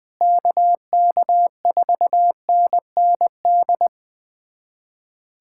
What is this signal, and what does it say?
Morse code, heard as a keyed tone, about 20 wpm, 700 Hz. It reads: KK4NND
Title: USA Callsign DE Callsign 20wpm